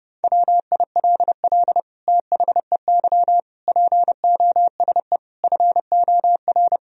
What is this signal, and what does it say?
Morse code, heard as a keyed tone, about 32 wpm, 700 Hz. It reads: WILL THEY POSE FOR
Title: Sentences from Top 1000 Words - Review Part 2 - 30wpm